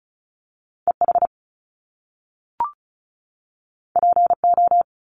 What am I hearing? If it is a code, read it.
Morse code, heard as a keyed tone, about 36 wpm, 700 Hz. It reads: EH PO
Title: Top Two Letter Combinations in Words 35wpm